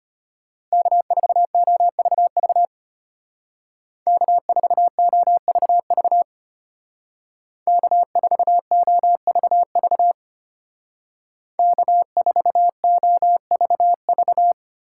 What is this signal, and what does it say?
Morse code, heard as a keyed tone, about 32 wpm, 700 Hz. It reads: K4OVV K4OVV K4OVV K4OVV